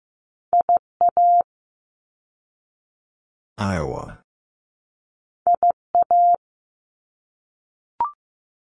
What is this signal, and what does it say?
Morse code, heard as a keyed tone, about 15 wpm, 700 Hz. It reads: IA IA
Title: ARRL Sections 15wpm